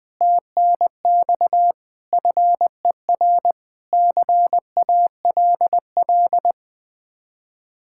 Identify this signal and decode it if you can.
Morse code, heard as a keyed tone, about 20 wpm, 700 Hz. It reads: TNX FER CALL
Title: Speed-Racing Common QSO Phrases - 20wpm